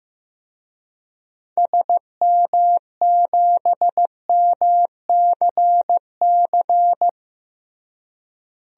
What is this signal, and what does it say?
Morse code, heard as a keyed tone, about 15 wpm, 700 Hz. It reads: SM7MCC
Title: DXCC-100 Call Signs - 6 Characters - Repeated 2x - 15wpm